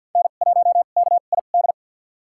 Morse code, heard as a keyed tone, about 52 wpm, spaced as for 36 wpm, 700 Hz. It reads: N1KID
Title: USA Call Signs - 2x Character Spacing - 50wpm